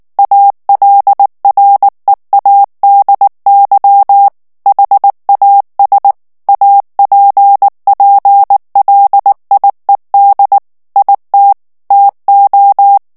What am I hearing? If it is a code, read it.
Morse code, heard as a keyed tone, about 19 wpm, 800 Hz. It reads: ALREADY HAS APPLIED IT TO
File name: ANGLAIS texte clair 3 vitesse 18 mots minute